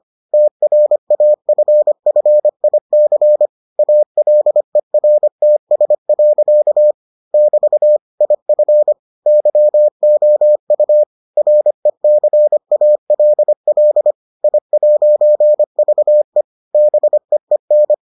Morse code, heard as a keyed tone, about 25 wpm, 600 Hz. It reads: TRAFFIC ALERTS. <BT> IF YOU RECALL I'VE BEEN